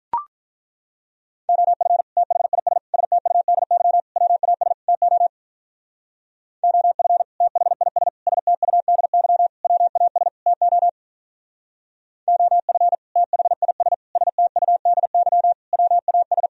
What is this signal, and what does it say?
Morse code, heard as a keyed tone, about 43 wpm, 700 Hz. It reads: OF THIS STUDY WAS TO OF THIS STUDY WAS TO OF THIS STUDY WAS
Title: Speed-Racing - Sets of 5 Words 35wpm